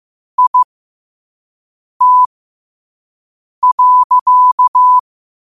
Morse code, heard as a keyed tone, about 15 wpm, spaced as for 5 wpm, 1000 Hz. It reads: IT.
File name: qotdfiles / 5wpm